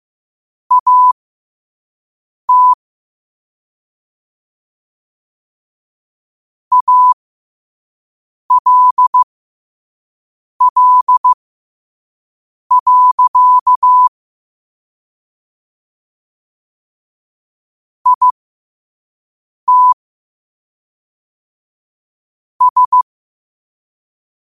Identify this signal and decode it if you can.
Morse code, heard as a keyed tone, about 15 wpm, spaced as for 5 wpm, 1000 Hz. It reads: AT ALL. IT S
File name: qotdfiles / 5wpm